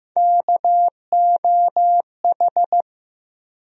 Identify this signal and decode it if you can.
Morse code, heard as a keyed tone, about 15 wpm, 700 Hz. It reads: KOH
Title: USA Call Sign Postfixes - 3 Letters 15wpm